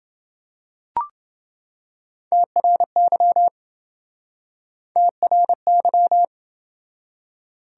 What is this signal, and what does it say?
Morse code, heard as a keyed tone, about 29 wpm, 700 Hz. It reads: TRY TRY